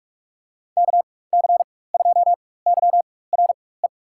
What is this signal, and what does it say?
Morse code, heard as a keyed tone, about 47 wpm, spaced as for 21 wpm, 700 Hz. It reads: KC2YRE
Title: USA Call Signs - 6 Characters - 4x Character Spacing - 45wpm